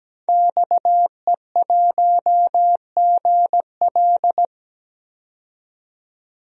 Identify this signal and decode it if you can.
Morse code, heard as a keyed tone, about 17 wpm, 700 Hz. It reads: XE1GL
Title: DXCC-100 Call Signs - 5 Characters - 17wpm